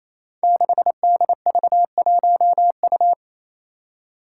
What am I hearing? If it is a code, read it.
Morse code, heard as a keyed tone, about 28 wpm, 700 Hz. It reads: 6DV1U
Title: Five Challenging Letter-Number Combinations 28wpm